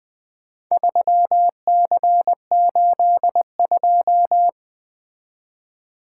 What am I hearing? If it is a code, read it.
Morse code, heard as a keyed tone, about 21 wpm, 700 Hz. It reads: 3C82
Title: Four Letter-Number Combinations - 20wpm